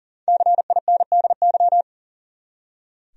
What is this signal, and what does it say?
Morse code, heard as a keyed tone, about 40 wpm, 700 Hz. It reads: CINDY
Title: USA Common Names - 40wpm